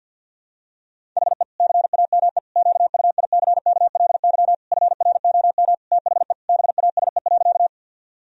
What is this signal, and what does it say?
Morse code, heard as a keyed tone, about 50 wpm, 700 Hz. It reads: HE CAME QUICKLY FROM THE BASE.
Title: Sentences from Top 1000 Words - Review Part 1 - 50wpm